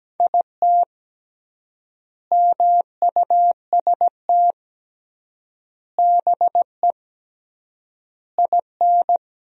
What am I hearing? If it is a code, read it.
Morse code, heard as a keyed tone, about 17 wpm, 700 Hz. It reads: IT MUST BE IN
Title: 3x Word-Spacing - Sentences from Top 300 Words - 17wpm